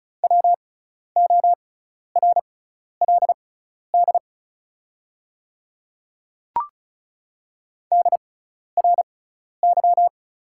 Morse code, heard as a keyed tone, about 36 wpm, spaced as for 12 wpm, 690 Hz. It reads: WORLD DRY
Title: 6x Character Spacing - Top 500 Words - 35wpm